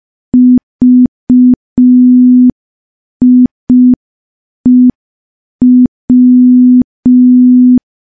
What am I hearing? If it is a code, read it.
Morse code, heard as a keyed tone, about 5 wpm, 250 Hz. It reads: VIEW
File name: morse / morse-training-05WPM